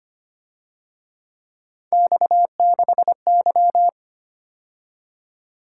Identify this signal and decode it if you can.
Morse code, heard as a keyed tone, about 29 wpm, 700 Hz. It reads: X6Y
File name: Speed-Racing - Three Letter-Number Combinations - 25wpm